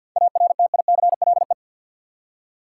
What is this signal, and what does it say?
Morse code, heard as a keyed tone, about 53 wpm, 700 Hz. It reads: ARTICLE
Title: Seven Letter Words - 50wpm